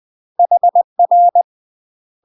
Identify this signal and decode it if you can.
Morse code, heard as a keyed tone, about 20 wpm, 700 Hz. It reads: HR
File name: UK MASTER QSOs TRAINING / 20WPM / UK MASTER QSOs 0050